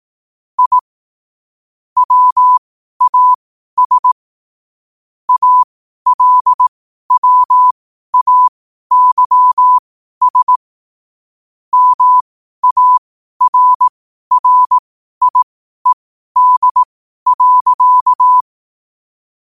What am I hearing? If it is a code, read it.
Morse code, heard as a keyed tone, about 18 wpm, spaced as for 12 wpm, 1000 Hz. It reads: I WAS ALWAYS MARRIED.
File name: qotdfiles / 12wpm